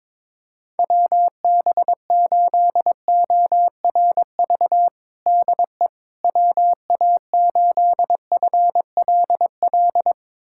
Morse code, heard as a keyed tone, about 22 wpm, 700 Hz. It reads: WB8ORV DE WA8FLL
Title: USA Callsign DE Callsign - 6 Characters - 22wpm